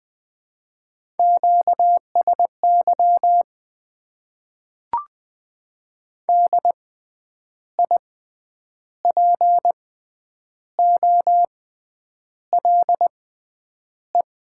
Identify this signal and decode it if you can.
Morse code, heard as a keyed tone, about 20 wpm, spaced as for 8 wpm, 700 Hz. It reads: QSY DIPOLE
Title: Top 90 QSO Elements - 6x Character Spacing - 20wpm